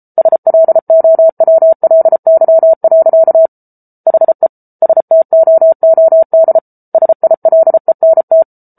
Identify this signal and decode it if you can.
Morse code, heard as a keyed tone, about 34 wpm, 650 Hz. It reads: SLOWLY. HE STOOD SILENT